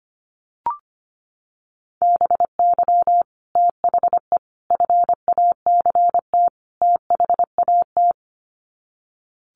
Morse code, heard as a keyed tone, about 25 wpm, 700 Hz. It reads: BY THE FACT THAT
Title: Sets of 4 Words 25wpm